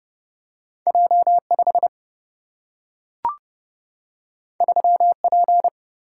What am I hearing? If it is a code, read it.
Morse code, heard as a keyed tone, about 29 wpm, 700 Hz. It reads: J5 3P
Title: Single-Letter Single-Number 30wpm